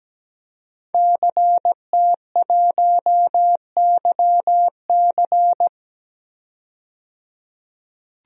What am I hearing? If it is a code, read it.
Morse code, heard as a keyed tone, about 17 wpm, 700 Hz. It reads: CT1YC
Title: DXCC-100 Call Signs - 5 Characters - 17wpm